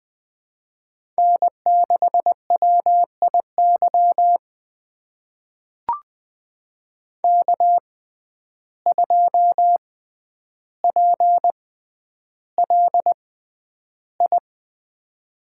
Morse code, heard as a keyed tone, about 20 wpm, spaced as for 10 wpm, 700 Hz. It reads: N6WIY K2PLI
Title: USA Call Signs - 5 Characters - 6x Character Spacing - 20wpm